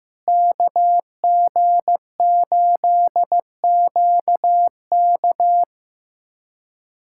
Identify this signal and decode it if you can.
Morse code, heard as a keyed tone, about 15 wpm, 700 Hz. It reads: KG8QK
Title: USA Callsign DE Callsign - 5 Characters - 15wpm